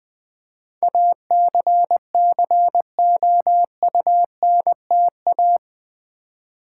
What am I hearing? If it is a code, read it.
Morse code, heard as a keyed tone, about 20 wpm, 700 Hz. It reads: ACCOUNTA
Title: Build A Word - 4 Letters At A Time - 11-Letter Words - 20wpm